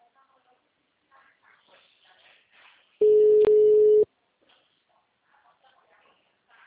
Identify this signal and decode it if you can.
Ringback tone (Europe, 425 Hz).
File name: out- -2015- - - .78802